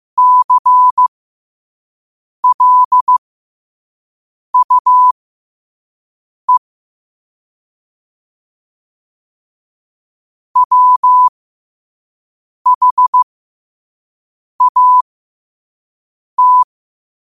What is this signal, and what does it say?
Morse code, heard as a keyed tone, about 15 wpm, spaced as for 5 wpm, 1000 Hz. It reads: CLUE WHAT
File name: qotdfiles / 5wpm